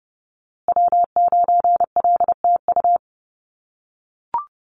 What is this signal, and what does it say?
Morse code, heard as a keyed tone, about 30 wpm, 700 Hz. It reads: W9LTU
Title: USA Call Signs Repeated 3x 30wpm